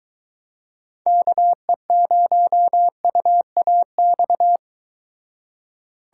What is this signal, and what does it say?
Morse code, heard as a keyed tone, about 23 wpm, 700 Hz. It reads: KE0UAX